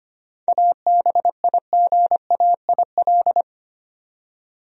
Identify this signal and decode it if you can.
Morse code, heard as a keyed tone, about 25 wpm, 700 Hz. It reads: ABIGAIL
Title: USA Common Names - 25wpm